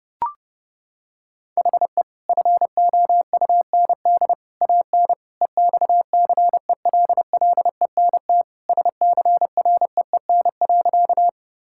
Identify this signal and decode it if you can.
Morse code, heard as a keyed tone, about 32 wpm, 700 Hz. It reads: HE FOUND AN EXCELLENT SCREEN.
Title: Sentences from Top 1000 Words - Review Part 6 - 30wpm